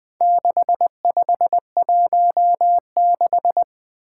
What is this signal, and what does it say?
Morse code, heard as a keyed tone, about 20 wpm, 700 Hz. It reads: 6516
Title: Four Challenging Letter-Number Combinations 20wpm